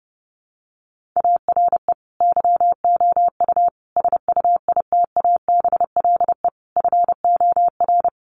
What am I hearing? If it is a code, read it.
Morse code, heard as a keyed tone, about 28 wpm, 700 Hz. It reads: ARE YOU SUITABLE FOR
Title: Sentences from Top 1400 Words 30wpm